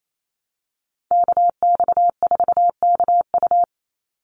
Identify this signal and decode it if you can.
Morse code, heard as a keyed tone, about 28 wpm, 700 Hz. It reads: KX4KU
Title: Call Signs - 28wpm